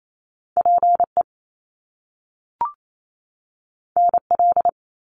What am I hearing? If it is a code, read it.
Morse code, heard as a keyed tone, about 28 wpm, 700 Hz. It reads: PE NL
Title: Canadian Provinces and Territories 28wpm